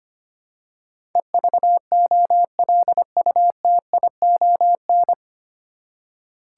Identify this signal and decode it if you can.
Morse code, heard as a keyed tone, about 25 wpm, 700 Hz. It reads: EVOLUTION
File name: Nine Letter Words - 25wpm